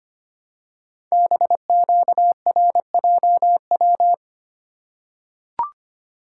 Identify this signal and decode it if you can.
Morse code, heard as a keyed tone, about 25 wpm, 700 Hz. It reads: BQRJW
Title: Code Groups - 5 Letters - 25wpm